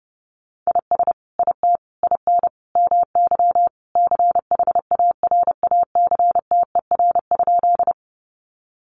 Morse code, heard as a keyed tone, about 30 wpm, 700 Hz. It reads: IS IT IN MY CHARACTER?
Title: Sentences from Top 1200 Words 30wpm